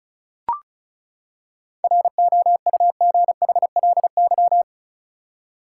Morse code, heard as a keyed tone, about 34 wpm, 700 Hz. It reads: ROUGHLY
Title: Seven Letter Words - 35wpm